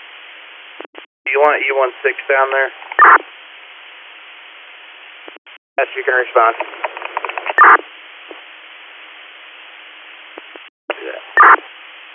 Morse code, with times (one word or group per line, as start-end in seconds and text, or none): none